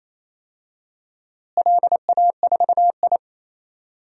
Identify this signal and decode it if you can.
Morse code, heard as a keyed tone, about 28 wpm, 700 Hz. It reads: LA4I
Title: DXCC-100 Call Signs - 4 Characters - Repeated 3x - 28wpm